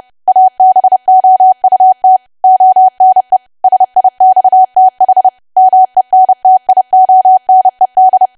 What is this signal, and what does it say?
Morse code, heard as a keyed tone, about 30 wpm, 750 Hz. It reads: ABOUT ONE SIXTH MENTIONED
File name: cwa / W1AW-Level3 / 30WPM